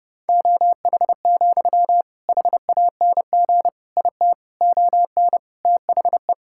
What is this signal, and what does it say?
Morse code, heard as a keyed tone, about 30 wpm, 700 Hz. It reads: OH, HANG IT ON THE
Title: Sentences from Top 1600 Words 30wpm